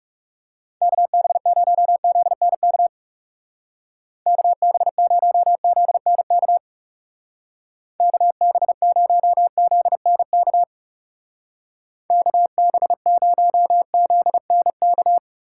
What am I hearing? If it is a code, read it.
Morse code, heard as a keyed tone, about 36 wpm, 700 Hz. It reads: KB0ZNK KB0ZNK KB0ZNK KB0ZNK
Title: Speed Racing - USA 2x3 Call Signs 30wpm